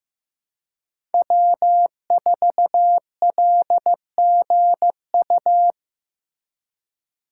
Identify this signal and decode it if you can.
Morse code, heard as a keyed tone, about 15 wpm, 700 Hz. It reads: W4LGU
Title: Code Groups - 5 Letter:Numbers - 15wpm